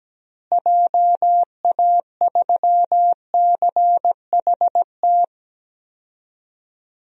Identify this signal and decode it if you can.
Morse code, heard as a keyed tone, about 17 wpm, 700 Hz. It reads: JA3CHT
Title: DXCC-100 Call Signs - 6 Characters - Repeated 3x - 17wpm